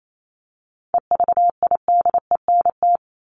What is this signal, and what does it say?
Morse code, heard as a keyed tone, about 28 wpm, 700 Hz. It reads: EVIDENT
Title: LY Suffix - Top 100 Words with Base Words 28wpm